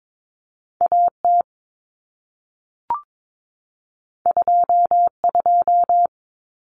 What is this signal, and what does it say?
Morse code, heard as a keyed tone, about 22 wpm, 700 Hz. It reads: AT 22
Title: Common Cut Numbers - 2-Digit 22wpm